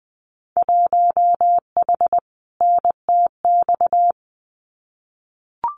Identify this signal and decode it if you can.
Morse code, heard as a keyed tone, about 20 wpm, 700 Hz. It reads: 1H NTX
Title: Winter Field Day Exchanges 20wpm